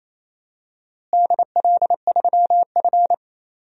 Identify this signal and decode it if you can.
Morse code, heard as a keyed tone, about 28 wpm, 700 Hz. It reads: DL3F
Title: DXCC-100 Call Signs - 4 Characters - Repeated 3x - 28wpm